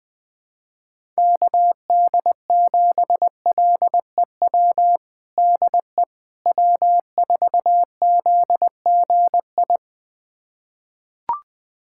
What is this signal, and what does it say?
Morse code, heard as a keyed tone, about 21 wpm, 690 Hz. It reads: KD7LEW DE W4ZGI
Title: USA Callsign DE Callsign 20wpm